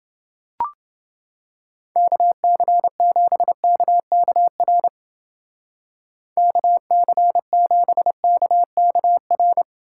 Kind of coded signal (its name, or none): Morse code